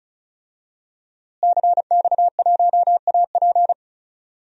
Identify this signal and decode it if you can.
Morse code, heard as a keyed tone, about 36 wpm, 700 Hz. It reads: CX1AP
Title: DXCC-100 Call Signs - 5 Characters - Repeated 2x - 35wpm